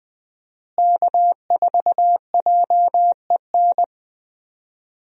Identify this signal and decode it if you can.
Morse code, heard as a keyed tone, about 20 wpm, 700 Hz. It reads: K4JEN
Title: USA Call Signs - 5 Characters - Repeated 3x - 20wpm